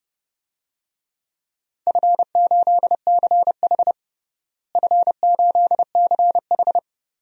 Morse code, heard as a keyed tone, about 32 wpm, 700 Hz. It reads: F8CH F8CH
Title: DXCC-100 Call Signs - 4 Characters - Repeated 3x - 30wpm